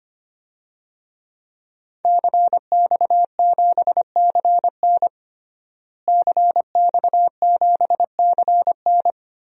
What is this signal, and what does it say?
Morse code, heard as a keyed tone, about 25 wpm, 700 Hz. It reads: CX7CN CX7CN
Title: DXCC-100 Call Signs - 5 Characters - Repeated 2x - 25wpm